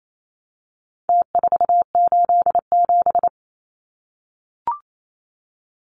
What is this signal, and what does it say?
Morse code, heard as a keyed tone, about 28 wpm, 700 Hz. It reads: T487
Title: Numbers - Contest Style - 4-Digit 28wpm